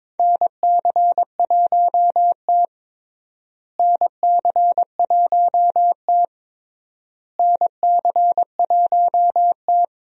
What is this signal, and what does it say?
Morse code, heard as a keyed tone, about 22 wpm, 700 Hz. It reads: NC1T NC1T NC1T
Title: USA Call Signs - 4 Characters - Repeated 3x - 22wpm